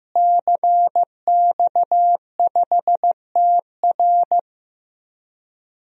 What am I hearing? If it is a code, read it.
Morse code, heard as a keyed tone, about 15 wpm, 700 Hz. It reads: CX5TR
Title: DXCC-100 Call Signs - 5 Characters - Repeated 3x - 15wpm